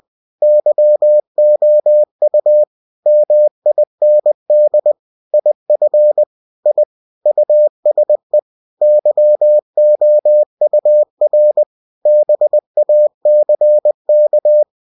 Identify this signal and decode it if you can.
Morse code, heard as a keyed tone, about 20 wpm, 600 Hz. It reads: YOU MIND IF I USE YOUR BACK
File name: cw-books / winnie-the-pooh / winnie-20wpm-page-0019